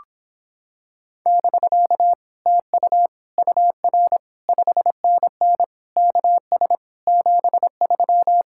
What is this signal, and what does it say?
Morse code, heard as a keyed tone, about 26 wpm, 700 Hz. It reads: <BK> TU UR 5NN KS 73